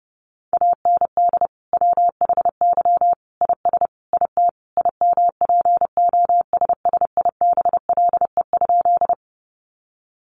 Morse code, heard as a keyed tone, about 30 wpm, 700 Hz. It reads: AND WHY IS IT IMPOSSIBLE?
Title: Sentences from Top 1100 Words 30wpm